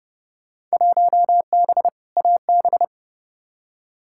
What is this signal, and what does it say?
Morse code, heard as a keyed tone, about 32 wpm, 700 Hz. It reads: 1B AB
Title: Field Day Exchanges - 30wpm